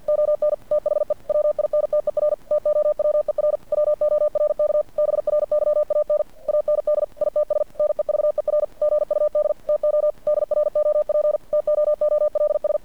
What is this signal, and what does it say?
Morse code, heard as a keyed tone, about 50 wpm, 600 Hz. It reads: ON THE WINTER TOWER WORK BRYAN AND ITS NEVER GUD TO DROP TOOLS
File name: archived / CRSnet